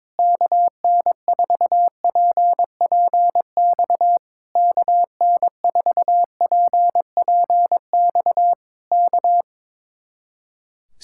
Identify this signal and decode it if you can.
Morse code, heard as a keyed tone, about 21 wpm, 700 Hz. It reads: KN4PPX KN4PPX K
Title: Speed-Racing Common QSO Phrases - 22wpm